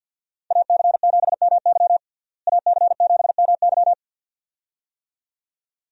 Morse code, heard as a keyed tone, about 47 wpm, 700 Hz. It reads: AC7MY AC7MY
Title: USA Call Signs - 5 Characters - Repeated 2x - 50wpm